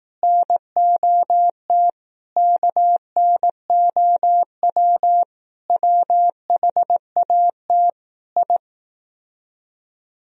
Morse code, heard as a keyed tone, about 18 wpm, 700 Hz. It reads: NOT KNOW WHAT I
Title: Speed-Racing - Sets of 5 Words 15wpm